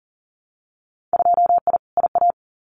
Morse code, heard as a keyed tone, about 40 wpm, 700 Hz. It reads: 2I IA
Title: Winter Field Day Exchanges Encore 40wpm